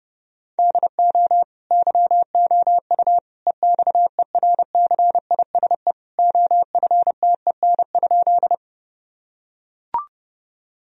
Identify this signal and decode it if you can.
Morse code, heard as a keyed tone, about 30 wpm, 700 Hz. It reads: DO YOU EXERCISE OFTEN?
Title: Sentences from Top 1000 Words - Review Part 4 - 30wpm